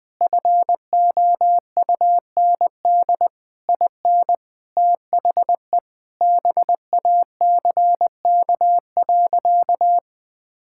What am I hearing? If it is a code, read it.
Morse code, heard as a keyed tone, about 20 wpm, 700 Hz. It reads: FOUND IN THE BACK.
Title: Sentences from Top 1000 Words - Review Part 5 - 20wpm